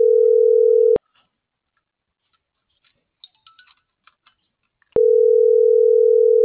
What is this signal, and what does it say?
Ringback tone (North America), 440 + 480 Hz.